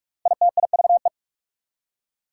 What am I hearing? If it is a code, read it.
Morse code, heard as a keyed tone, about 44 wpm, 690 Hz. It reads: ITIVE
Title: Prefixes and Suffixes 45wpm